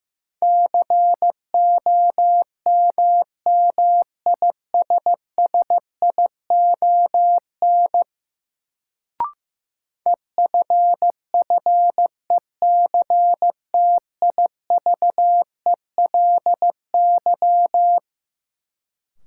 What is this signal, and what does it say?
Morse code, heard as a keyed tone, about 15 wpm, 700 Hz. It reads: COMMISSION EFFECTIVELY
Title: Sentences from Top 1400 Words - New Word Drill - 15wpm